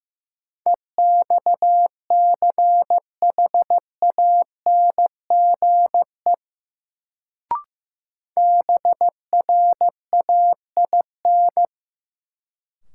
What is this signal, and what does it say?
Morse code, heard as a keyed tone, about 15 wpm, 700 Hz. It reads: EXCHANGE BRAIN
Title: Sentences from Top 1000 Words - Review Part 6 - New Word Drill - 15wpm